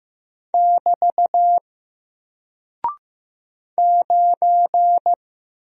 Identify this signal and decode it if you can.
Morse code, heard as a keyed tone, about 15 wpm, 700 Hz. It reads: <BT> 9
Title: Single Character (Letter:Number:Punctuation:Prosign) Alt 15wpm